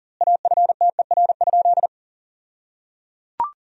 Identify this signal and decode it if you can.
Morse code, heard as a keyed tone, about 37 wpm, 700 Hz. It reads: AFTER?
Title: 2x Word-Spacing - Sentences from Top 200 Words - 40wpm